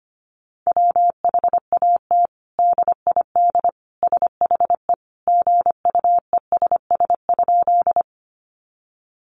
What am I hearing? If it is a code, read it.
Morse code, heard as a keyed tone, about 25 wpm, 700 Hz. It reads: WHAT DID SHE GUESS?
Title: Sentences from Top 1500 Words 25wpm